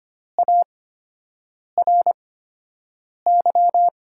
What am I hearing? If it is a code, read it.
Morse code, heard as a keyed tone, about 25 wpm, spaced as for 7 wpm, 700 Hz. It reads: ARY
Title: USA Call Signs - 5 Characters - 8x Character Spacing - 25wpm